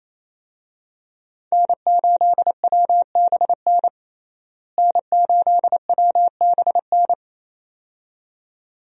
Morse code, heard as a keyed tone, about 28 wpm, 700 Hz. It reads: N8WBN N8WBN
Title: USA Call Signs - 5 Characters - Repeated 2x - 28wpm